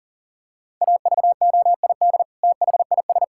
Morse code, heard as a keyed tone, about 40 wpm, 700 Hz. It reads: AVOID THIS
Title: Sentences from Top 1000 Words - Review Part 6 - 40wpm